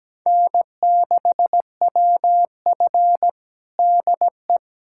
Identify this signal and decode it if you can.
Morse code, heard as a keyed tone, about 17 wpm, 700 Hz. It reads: N6WF DE
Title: USA Callsign DE Callsign - 4 Characters - 17wpm